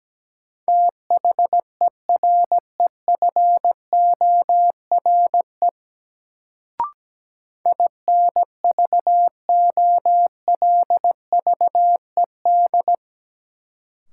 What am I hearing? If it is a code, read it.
Morse code, heard as a keyed tone, about 17 wpm, 700 Hz. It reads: THEREFORE INVOLVED
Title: Sentences from Top 800 Words - New Word Drill - 17wpm